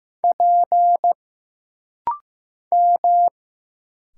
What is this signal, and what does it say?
Morse code, heard as a keyed tone, about 15 wpm, 700 Hz. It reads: P M
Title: Single Letters - Rapid-Fire 15wpm